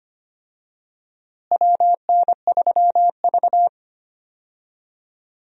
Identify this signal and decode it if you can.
Morse code, heard as a keyed tone, about 25 wpm, 700 Hz. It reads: WN3V
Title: USA Call Signs - 4 Characters - Repeated 3x - 25wpm